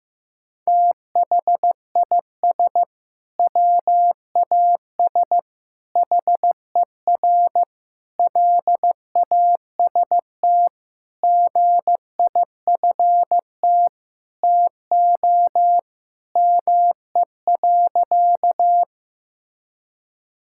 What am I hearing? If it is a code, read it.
Morse code, heard as a keyed tone, about 15 wpm, 700 Hz. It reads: THIS WAS HER LAST GIFT TO ME.
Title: Sentences from Top 1000 Words - Review Part 6 - 15wpm